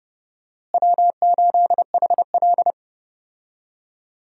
Morse code, heard as a keyed tone, about 30 wpm, 700 Hz. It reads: W8HL
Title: USA Call Signs - 4 Characters - Repeated 2x - 30wpm